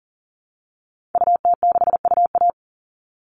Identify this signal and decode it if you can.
Morse code, heard as a keyed tone, about 40 wpm, 700 Hz. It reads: UT6UA
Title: Call Signs Repeated 2x - 40wpm